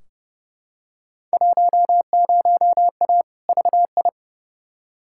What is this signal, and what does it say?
Morse code, heard as a keyed tone, about 30 wpm, 700 Hz. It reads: 10A VI
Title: Field Day Exchanges Encore - 30wpm